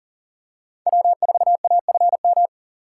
Morse code, heard as a keyed tone, about 40 wpm, 700 Hz. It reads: W4AFM
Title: USA Call Signs - 6x Character Spacing - 40wpm